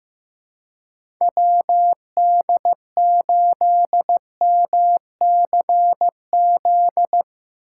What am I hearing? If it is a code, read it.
Morse code, heard as a keyed tone, about 15 wpm, 700 Hz. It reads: WD8MCZ
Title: USA Call Signs - 6 Characters - 3x Character Spacing - 15wpm